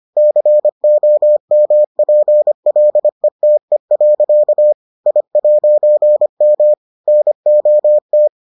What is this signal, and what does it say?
Morse code, heard as a keyed tone, about 25 wpm, 600 Hz. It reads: COMPLETE. I'M NOT